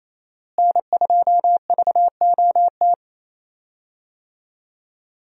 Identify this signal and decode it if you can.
Morse code, heard as a keyed tone, about 28 wpm, 700 Hz. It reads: N2VOT
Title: USA Call Signs - 5 Characters - Repeated 3x - 28wpm